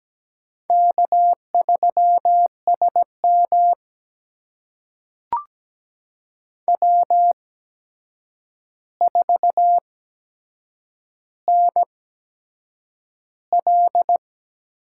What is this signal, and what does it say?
Morse code, heard as a keyed tone, about 17 wpm, spaced as for 7 wpm, 700 Hz. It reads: K3SM W4NL
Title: USA Call Signs - 8x Character Spacing - 17wpm